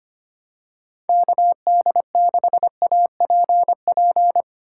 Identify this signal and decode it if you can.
Morse code, heard as a keyed tone, about 25 wpm, 700 Hz. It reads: KD6APP
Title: USA Call Signs - 6 Characters - 3x Character Spacing - 25wpm